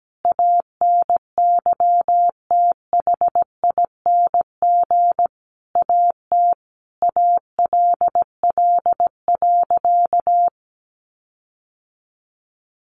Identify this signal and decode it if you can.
Morse code, heard as a keyed tone, about 17 wpm, 700 Hz. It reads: ANYTHING AT ALL.
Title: Sentences from Top 1300 Words 17wpm